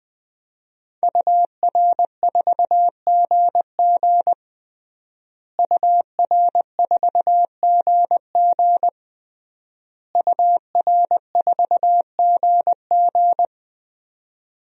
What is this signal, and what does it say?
Morse code, heard as a keyed tone, about 20 wpm, 700 Hz. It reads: UR4GG UR4GG UR4GG
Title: DXCC-100 Call Signs - 5 Characters - Repeated 3x - 20wpm